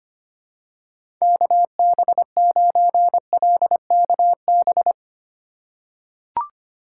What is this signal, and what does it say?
Morse code, heard as a keyed tone, about 25 wpm, 700 Hz. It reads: KB9LKB
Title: USA Call Signs - 6 Characters - 3x Character Spacing - 25wpm